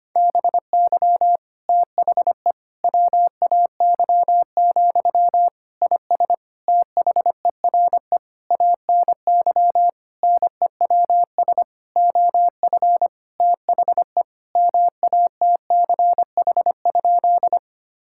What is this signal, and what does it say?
Morse code, heard as a keyed tone, about 25 wpm, 700 Hz. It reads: BY THE WAY, IS THERE ANY NEWS OF THE MATCH?
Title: Sentences from Top 1600 Words 25wpm